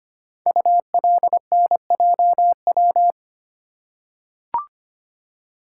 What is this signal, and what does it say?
Morse code, heard as a keyed tone, about 25 wpm, 700 Hz. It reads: ULNJW
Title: Code Groups - 5 Letters - 25wpm